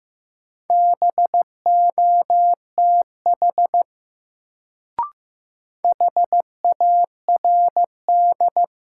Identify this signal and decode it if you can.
Morse code, heard as a keyed tone, about 15 wpm, 700 Hz. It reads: BOTH HARD
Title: Sentences from Top 300 Words - New Word Drill - 15wpm